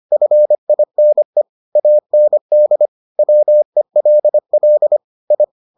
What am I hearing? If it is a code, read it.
Morse code, heard as a keyed tone, about 25 wpm, 600 Hz. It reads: FINE AND WELL I